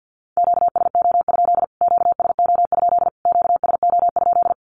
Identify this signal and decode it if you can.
Morse code, heard as a keyed tone, about 48 wpm, 700 Hz. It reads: QSO? QSO? QSO?
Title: Q-Codes Repeated 3x 50wpm